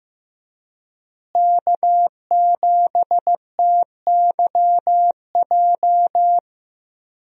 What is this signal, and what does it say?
Morse code, heard as a keyed tone, about 15 wpm, 700 Hz. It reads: K7TYJ
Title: USA Call Signs - 5 Characters - Repeated 2x - 15wpm